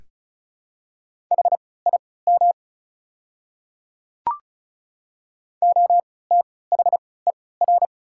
Morse code, heard as a keyed tone, about 36 wpm, spaced as for 20 wpm, 690 Hz. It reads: HIM OTHER
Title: 3x Character Spacing - Top 200 Words - 35wpm